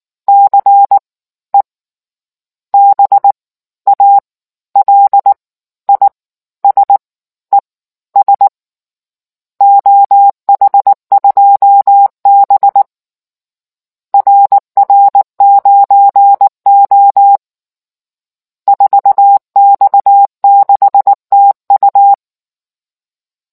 Morse code, heard as a keyed tone, about 19 wpm, spaced as for 15 wpm, 800 Hz. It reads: CE BALISES OH2B RR9O 4X6TU